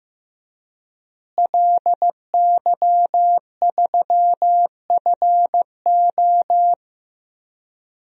Morse code, heard as a keyed tone, about 16 wpm, 700 Hz. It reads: LY3FO
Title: DXCC-100 Call Signs - 5 Characters - Repeated 2x - 15wpm